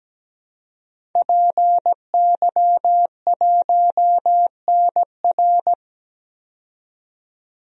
Morse code, heard as a keyed tone, about 17 wpm, 700 Hz. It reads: PY1NR
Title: DXCC-100 Call Signs - 5 Characters - 17wpm